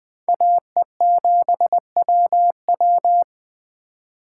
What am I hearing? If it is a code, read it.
Morse code, heard as a keyed tone, about 21 wpm, 700 Hz. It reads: AE7WW